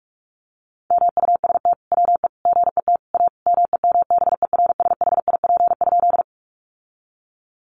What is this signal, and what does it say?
Morse code, heard as a keyed tone, about 44 wpm, 700 Hz. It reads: MUST WE GET A MEMBERSHIP?
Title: Sentences from Top 1400 Words 45wpm